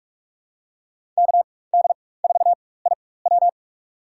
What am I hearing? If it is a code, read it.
Morse code, heard as a keyed tone, about 42 wpm, spaced as for 21 wpm, 700 Hz. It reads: KD4IW
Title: USA Call Signs - 5 Characters - 4x Character Spacing - 45wpm